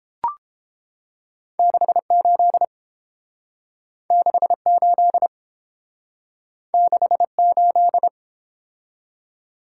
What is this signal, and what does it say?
Morse code, heard as a keyed tone, about 29 wpm, 700 Hz. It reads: 68 68 68